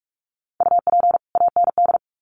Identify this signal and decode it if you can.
Morse code, heard as a keyed tone, about 44 wpm, 700 Hz. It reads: UP AND
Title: Sets of 2 Words 45wpm